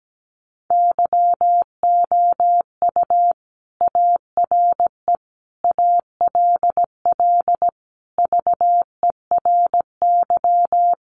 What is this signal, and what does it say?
Morse code, heard as a keyed tone, about 17 wpm, 700 Hz. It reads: YOU ARE ALL VERY
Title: Sentences from Top 200 Words 17wpm